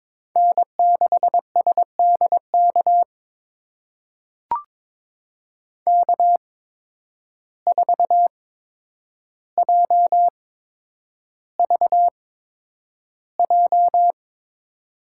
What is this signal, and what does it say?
Morse code, heard as a keyed tone, about 22 wpm, spaced as for 9 wpm, 700 Hz. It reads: N6SDK K4JVJ
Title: USA Call Signs - 5 Characters - 8x Character Spacing - 22wpm